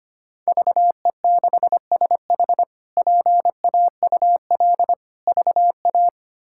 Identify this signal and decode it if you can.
Morse code, heard as a keyed tone, about 25 wpm, 700 Hz. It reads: VE6SH PAUL VA
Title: K1USN SST QSO Practice - 25wpm